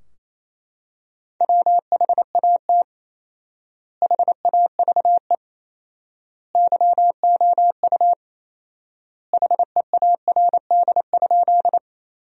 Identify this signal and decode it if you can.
Morse code, heard as a keyed tone, about 28 wpm, 700 Hz. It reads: WHAT HAVE YOU HEARD?
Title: 4x Word-Spacing - Sentences from Top 400 Words - 28wpm